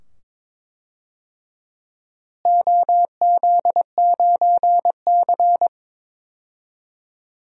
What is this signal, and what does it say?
Morse code, heard as a keyed tone, about 22 wpm, 700 Hz. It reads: OZ9C
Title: DXCC-100 Call Signs - 4 Characters - Repeated 2x - 22wpm